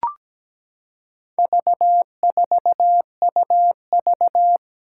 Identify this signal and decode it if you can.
Morse code, heard as a keyed tone, about 17 wpm, 700 Hz. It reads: V4UV
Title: Four Challenging Letter-Number Combinations 17wpm